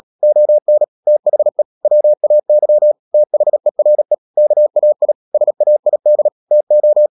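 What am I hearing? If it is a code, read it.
Morse code, heard as a keyed tone, about 37 wpm, 600 Hz. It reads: ON THE WAY THERE KAI SAID TO
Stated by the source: Morse code practice at 35–39 wpm sending English text of Ken Follett's novel Never, from CW med Gnister.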